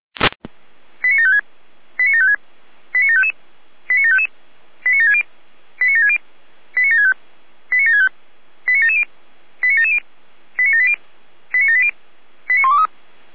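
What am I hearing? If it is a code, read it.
Five-tone selective call (ZVEI), heard as quick runs of five tones, tones of about 80 ms. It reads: 89756 89756 89755 89755 89760 89760 89765 89765 89770 89770 89780 89780 89123
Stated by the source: Five-tone ZVEI selective call received by SDR.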